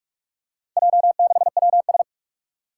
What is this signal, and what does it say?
Morse code, heard as a keyed tone, about 42 wpm, 700 Hz. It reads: J6WS
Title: Four Challenging Letter-Number Combinations 45wpm